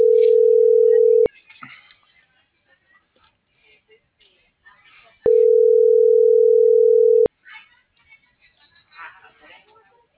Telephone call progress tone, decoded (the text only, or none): ringback tone